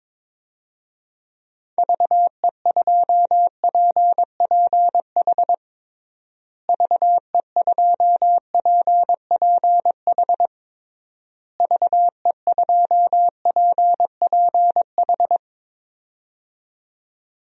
Morse code, heard as a keyed tone, about 22 wpm, 700 Hz. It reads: VE2PPH VE2PPH VE2PPH
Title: DXCC-100 Call Signs - 6 Characters - Repeated 3x - 22wpm